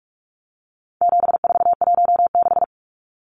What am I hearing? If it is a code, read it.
Morse code, heard as a keyed tone, about 44 wpm, 700 Hz. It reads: 7416
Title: Numbers - 4-Digit 45wpm